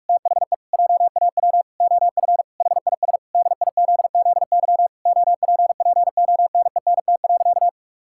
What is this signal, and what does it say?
Morse code, heard as a keyed tone, about 46 wpm, 700 Hz. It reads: THE JAW OF HIS DIZZY OPPONENT.
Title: Pangram Challenge v2 - 45wpm